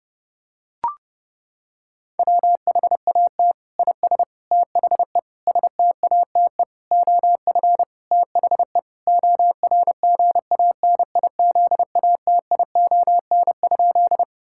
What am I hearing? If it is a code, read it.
Morse code, heard as a keyed tone, about 30 wpm, 700 Hz. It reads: WHAT IS THE STATE OF THE ORGANIZATION?
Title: Sentences from Top 1000 Words - Review Part 5 - 30wpm